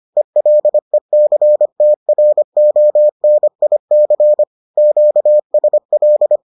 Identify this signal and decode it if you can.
Morse code, heard as a keyed tone, about 25 wpm, 600 Hz. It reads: ELECTRONIC QSL